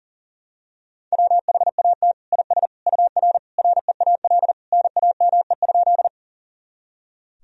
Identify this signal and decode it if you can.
Morse code, heard as a keyed tone, about 37 wpm, 700 Hz. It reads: WHAT IS UR REAL NAME?
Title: Sentences from Top 300 Words and Abbreviations 40wpm